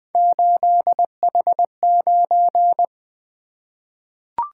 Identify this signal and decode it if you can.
Morse code, heard as a keyed tone, about 20 wpm, 700 Hz. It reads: 8H9
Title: Three Letter-Number Combinations - 20wpm